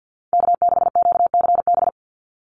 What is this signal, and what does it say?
Morse code, heard as a keyed tone, about 50 wpm, 700 Hz. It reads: K6QCB
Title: NA SOTA Chaser Callsigns 50wpm